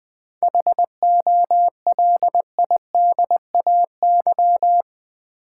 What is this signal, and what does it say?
Morse code, heard as a keyed tone, about 20 wpm, 700 Hz. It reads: HOLIDAY
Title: Seven Letter Words - 20wpm